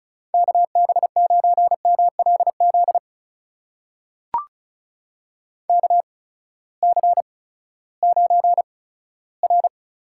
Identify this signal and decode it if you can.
Morse code, heard as a keyed tone, about 36 wpm, spaced as for 18 wpm, 700 Hz. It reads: KB9MLZ KC9R
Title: USA Call Signs - 8x Character Spacing - 35wpm